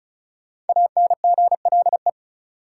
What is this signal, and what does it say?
Morse code, heard as a keyed tone, about 34 wpm, 700 Hz. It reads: ANGLE
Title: Sentences from Top 1500 Words - New Word Drill Alt - 35wpm